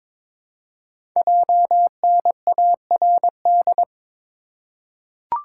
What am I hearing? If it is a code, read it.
Morse code, heard as a keyed tone, about 22 wpm, 700 Hz. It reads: JNARD
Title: Code Groups - 5 Letters - 22wpm